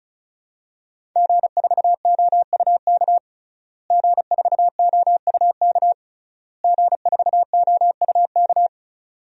Morse code, heard as a keyed tone, about 36 wpm, 700 Hz. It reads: G4OUK G4OUK G4OUK
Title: DXCC-100 Call Signs - 5 Characters - Repeated 3x - 35wpm